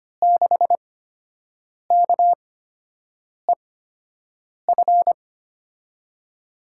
Morse code, heard as a keyed tone, about 25 wpm, spaced as for 7 wpm, 700 Hz. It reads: 6KEF
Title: USA Call Signs - 6 Characters - 8x Character Spacing - 25wpm